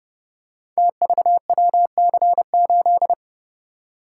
Morse code, heard as a keyed tone, about 28 wpm, 700 Hz. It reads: TVWC8
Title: Code Groups - 5 Letter:Numbers - 30wpm